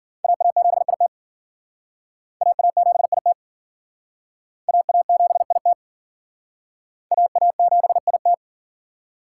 Morse code, heard as a keyed tone, about 52 wpm, 700 Hz. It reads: AA7IT AA7IT AA7IT AA7IT